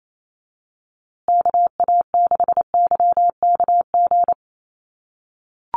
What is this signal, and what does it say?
Morse code, heard as a keyed tone, about 28 wpm, 700 Hz. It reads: KA6YKG
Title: USA Call Signs Repeated 2x 28wpm